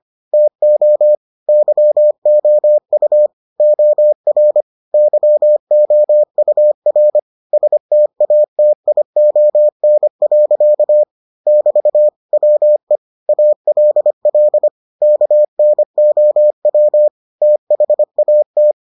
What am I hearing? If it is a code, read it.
Morse code, heard as a keyed tone, about 25 wpm, 600 Hz. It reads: TO YOU OR YOUR STATION. <BT> WE ALL KNOW THAT